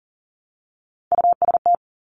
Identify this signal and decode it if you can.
Morse code, heard as a keyed tone, about 39 wpm, 700 Hz. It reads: UST
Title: Top Three Letter Combinations in Words 40wpm